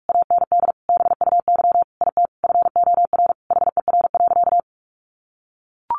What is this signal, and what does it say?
Morse code, heard as a keyed tone, about 44 wpm, 700 Hz. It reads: AND BUY IT FOR HER.
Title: Sentences from Top 1400 Words 45wpm